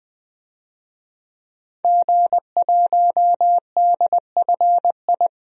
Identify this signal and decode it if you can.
Morse code, heard as a keyed tone, about 20 wpm, 700 Hz. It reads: G1DFI
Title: DXCC-100 Call Signs - 5 Characters - Repeated 3x - 20wpm